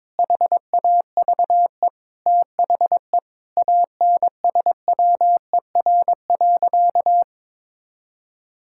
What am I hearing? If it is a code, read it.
Morse code, heard as a keyed tone, about 22 wpm, 700 Hz. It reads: HAVE THE ANSWER.
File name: practice / 22wpm / Sentences with 4-Word Phrases - 22wpm